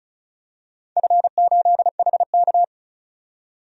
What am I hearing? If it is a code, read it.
Morse code, heard as a keyed tone, about 36 wpm, 690 Hz. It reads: F8HK
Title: DXCC-100 Call Signs - 4 Characters - 35wpm